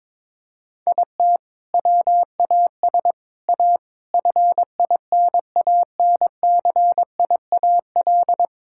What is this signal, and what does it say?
Morse code, heard as a keyed tone, about 21 wpm, 700 Hz. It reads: IT WAS A FINANCIAL
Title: Sentences from Top 1000 Words - Review Part 6 - 22wpm